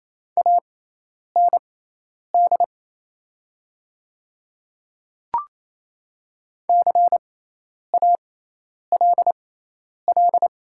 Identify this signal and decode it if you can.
Morse code, heard as a keyed tone, about 28 wpm, spaced as for 10 wpm, 700 Hz. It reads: AND CALL
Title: 6x Character Spacing - Top 100 Words - 28wpm